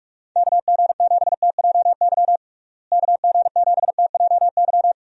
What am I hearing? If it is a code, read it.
Morse code, heard as a keyed tone, about 46 wpm, 700 Hz. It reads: KG7TJY KG7TJY
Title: USA Call Signs - 6 Characters - Repeated 3x - 45wpm